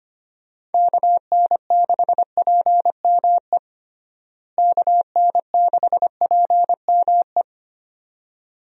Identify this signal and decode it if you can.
Morse code, heard as a keyed tone, about 25 wpm, 700 Hz. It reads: KN6PME KN6PME
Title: USA Call Signs - 6 Characters - Repeated 2x - 25wpm